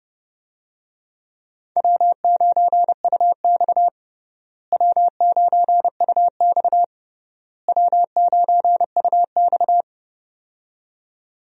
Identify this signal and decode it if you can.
Morse code, heard as a keyed tone, about 30 wpm, 700 Hz. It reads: W9UX W9UX W9UX
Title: USA Call Signs - 4 Characters - Repeated 3x - 30wpm